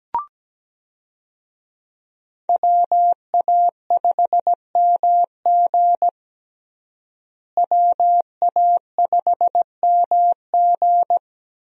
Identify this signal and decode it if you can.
Morse code, heard as a keyed tone, about 17 wpm, 700 Hz. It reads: WA5MG WA5MG
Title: USA Call Signs - 5 Characters - Repeated 2x - 17wpm